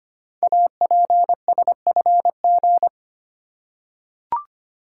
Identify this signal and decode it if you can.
Morse code, heard as a keyed tone, about 25 wpm, 690 Hz. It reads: APSFG
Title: Code Groups - 5 Letters - 25wpm